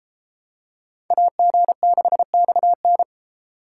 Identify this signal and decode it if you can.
Morse code, heard as a keyed tone, about 33 wpm, 700 Hz. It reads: AG6XN